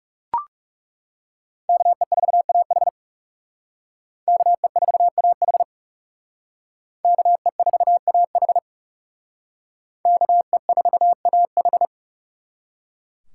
Morse code, heard as a keyed tone, about 38 wpm, 700 Hz. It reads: KE4AH KE4AH KE4AH KE4AH